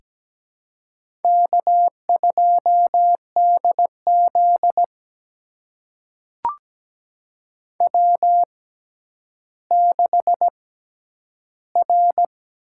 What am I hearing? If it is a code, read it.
Morse code, heard as a keyed tone, about 17 wpm, spaced as for 10 wpm, 700 Hz. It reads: K2DZ W6R
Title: USA Call Signs - 4 Characters - 6x Character Spacing - 17wpm